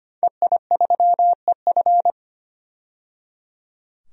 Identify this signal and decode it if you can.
Morse code, heard as a keyed tone, about 25 wpm, 700 Hz. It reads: EI3EF
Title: DXCC-100 Call Signs - 5 Characters - 25wpm